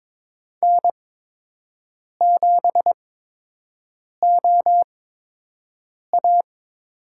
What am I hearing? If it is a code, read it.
Morse code, heard as a keyed tone, about 22 wpm, spaced as for 6 wpm, 700 Hz. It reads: N7OA
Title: USA Call Signs - 8x Character Spacing - 22wpm